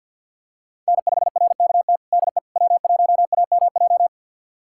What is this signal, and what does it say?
Morse code, heard as a keyed tone, about 51 wpm, 700 Hz. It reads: N5RKT DE W1AMJ
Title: USA Callsign DE Callsign - 5 Characters - 50wpm